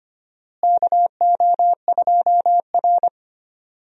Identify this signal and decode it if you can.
Morse code, heard as a keyed tone, about 25 wpm, 700 Hz. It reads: KO2R
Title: USA Call Signs - 4 Characters - Repeated 3x - 25wpm